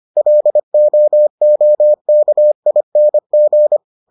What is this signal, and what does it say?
Morse code, heard as a keyed tone, about 25 wpm, 600 Hz. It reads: LOOKING